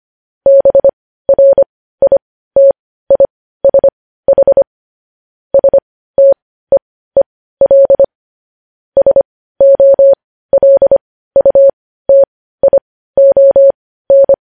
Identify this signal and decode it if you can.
Morse code, heard as a keyed tone, about 25 wpm, spaced as for 15 wpm, 550 Hz. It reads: BRITISH STEEL SOLUTION
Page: news-headlines-in-morse-code-at-15-wpm